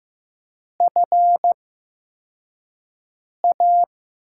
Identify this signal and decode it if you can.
Morse code, heard as a keyed tone, about 15 wpm, spaced as for 4 wpm, 700 Hz. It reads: FA
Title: USA Call Signs - 6 Characters - 8x Character Spacing - 15wpm